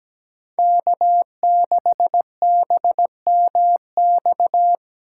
Morse code, heard as a keyed tone, about 17 wpm, 700 Hz. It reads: K6BMX
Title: USA Call Signs - 2x Character Spacing - 17wpm